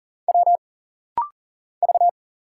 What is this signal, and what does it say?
Morse code, heard as a keyed tone, about 40 wpm, 700 Hz. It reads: W V
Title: Single Letters - Rapid-Fire 40wpm